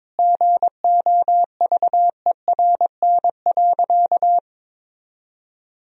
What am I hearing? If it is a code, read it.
Morse code, heard as a keyed tone, about 22 wpm, 700 Hz. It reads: GOVERN.
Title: Sentences from Top 1000 Words - Review Part 2 - 22wpm